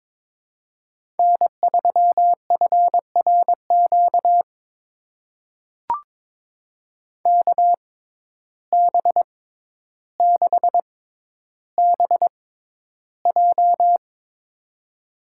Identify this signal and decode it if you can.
Morse code, heard as a keyed tone, about 22 wpm, spaced as for 11 wpm, 700 Hz. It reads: N3FRQ KB6BJ
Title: USA Call Signs - 6x Character Spacing - 22wpm